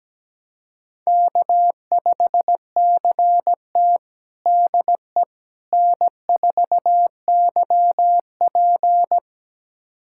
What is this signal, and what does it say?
Morse code, heard as a keyed tone, about 17 wpm, 700 Hz. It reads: K5CT DE N4YP
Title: USA Callsign DE Callsign - 4 Characters - 17wpm